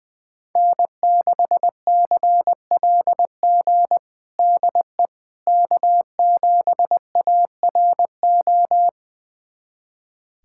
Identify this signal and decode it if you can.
Morse code, heard as a keyed tone, about 21 wpm, 700 Hz. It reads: N6CLG DE K7ARO
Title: USA Callsign DE Callsign - 5 Characters - 20wpm